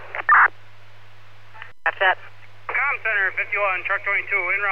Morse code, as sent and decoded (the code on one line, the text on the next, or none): none